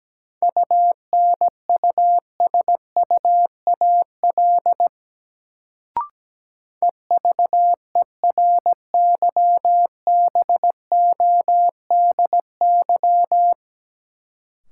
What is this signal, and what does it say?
Morse code, heard as a keyed tone, about 17 wpm, 700 Hz. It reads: UNUSUAL EVERYBODY
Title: Sentences from Top 1400 Words - New Word Drill Alt - 17wpm